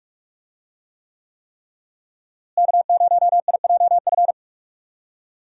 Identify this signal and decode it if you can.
Morse code, heard as a keyed tone, about 46 wpm, 700 Hz. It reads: K0IJF
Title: USA Call Signs - 5 Characters - Repeated 3x - 45wpm